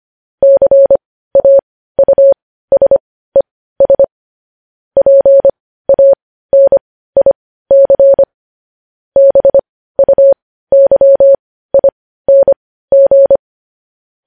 Morse code, heard as a keyed tone, about 25 wpm, spaced as for 15 wpm, 550 Hz. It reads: CAUSES PANIC BUYING
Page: news headlines in morse code at 15 wpm